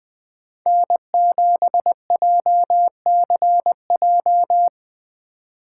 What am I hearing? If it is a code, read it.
Morse code, heard as a keyed tone, about 20 wpm, 700 Hz. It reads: N7JCJ